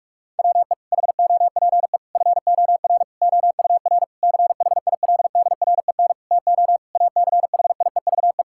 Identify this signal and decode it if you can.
Morse code, heard as a keyed tone, about 46 wpm, 700 Hz. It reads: WE HOPE FOR OUR CHILDREN TO ACHIEVE
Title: Sentences from Top 1000 Words - Review Part 6 - 45wpm